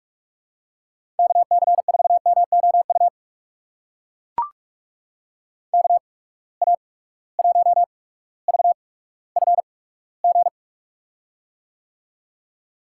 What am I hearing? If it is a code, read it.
Morse code, heard as a keyed tone, about 43 wpm, spaced as for 19 wpm, 700 Hz. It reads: KC4MOU KA1VFG
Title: USA Call Signs - 6 Characters - 8x Character Spacing - 45wpm